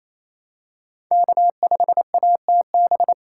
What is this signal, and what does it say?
Morse code, heard as a keyed tone, about 28 wpm, 700 Hz. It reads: K5ATB
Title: USA Callsign DE Callsign 28wpm